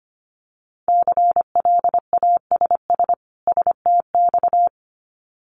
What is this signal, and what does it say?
Morse code, heard as a keyed tone, about 25 wpm, 700 Hz. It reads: CLASS STX
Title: Winter Field Day Exchanges 25wpm